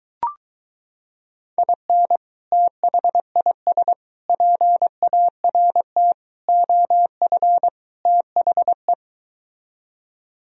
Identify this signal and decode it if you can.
Morse code, heard as a keyed tone, about 24 wpm, 700 Hz. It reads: IN THIS PART OF THE